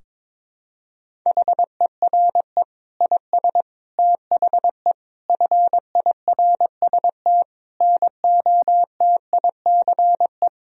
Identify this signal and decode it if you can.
Morse code, heard as a keyed tone, about 22 wpm, 700 Hz. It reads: HERE IS THE FIRST NOTICE
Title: Sentences from Top 1000 Words - Review Part 2 - 22wpm